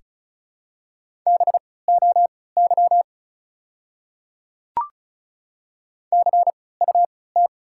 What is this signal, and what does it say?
Morse code, heard as a keyed tone, about 34 wpm, spaced as for 20 wpm, 700 Hz. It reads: BOY CUT
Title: 3x Character Spacing - Top 500 Words - 35wpm